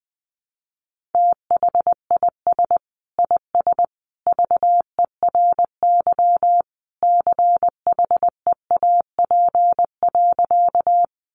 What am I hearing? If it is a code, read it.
Morse code, heard as a keyed tone, about 20 wpm, 700 Hz. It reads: THIS IS VERY CHEAP.
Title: Sentences from Top 1200 Words 20wpm